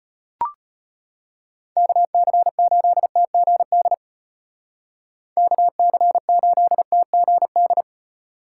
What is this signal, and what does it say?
Morse code, heard as a keyed tone, about 30 wpm, 700 Hz. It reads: KC8TGD KC8TGD
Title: Speed Racing - USA 2x3 Call Signs 25wpm